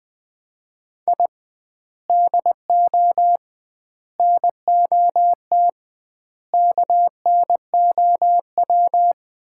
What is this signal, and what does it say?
Morse code, heard as a keyed tone, about 21 wpm, 700 Hz. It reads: I DO NOT KNOW
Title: 2x Word-Spacing - Sets of 5 Words - 20wpm